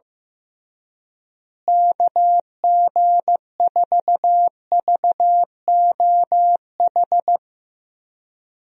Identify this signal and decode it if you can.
Morse code, heard as a keyed tone, about 15 wpm, 700 Hz. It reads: KG4VOH
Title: USA Call Signs - 6 Characters - Repeated 3x - 15wpm